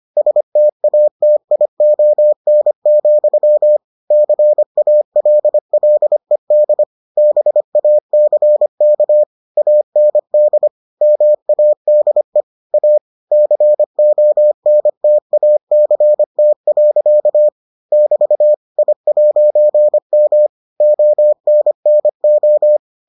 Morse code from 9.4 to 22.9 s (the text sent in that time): AND MADE A CONTACT. <BT> I'M ONNO